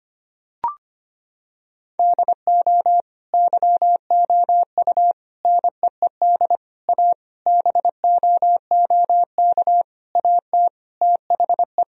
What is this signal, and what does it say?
Morse code, heard as a keyed tone, about 25 wpm, 700 Hz. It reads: DO YOU NEED A BOOK AT THE
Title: Sentences from Top 1000 Words - Review Part 4 - 25wpm